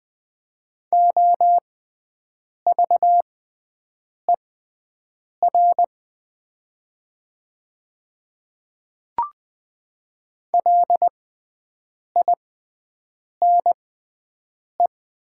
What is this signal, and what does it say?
Morse code, heard as a keyed tone, about 20 wpm, spaced as for 7 wpm, 700 Hz. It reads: OVER LINE
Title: 6x Character Spacing - Top 300 Words - 20wpm